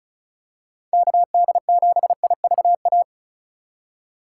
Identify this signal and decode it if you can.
Morse code, heard as a keyed tone, about 34 wpm, 700 Hz. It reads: KD7IVA
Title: USA Call Signs - 6 Characters - 4x Character Spacing - 35wpm